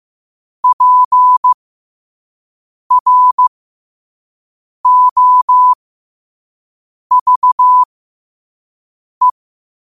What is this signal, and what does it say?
Morse code, heard as a keyed tone, about 15 wpm, spaced as for 5 wpm, 1000 Hz. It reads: PROVE
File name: qotdfiles / 5wpm